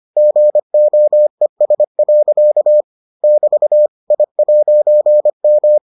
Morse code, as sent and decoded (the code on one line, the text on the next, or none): --. --- . ... .-.-.- / -...- / .. .----. --
GOES. <BT> I'M